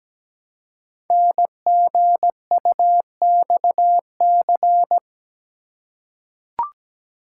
Morse code, heard as a keyed tone, about 17 wpm, 700 Hz. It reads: NGUXC
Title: Code Groups - 5 Letter:Numbers - 17wpm